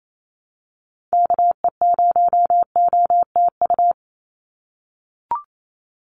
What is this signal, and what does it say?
Morse code, heard as a keyed tone, about 28 wpm, 700 Hz. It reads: KE0OTU
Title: USA Call Signs 28wpm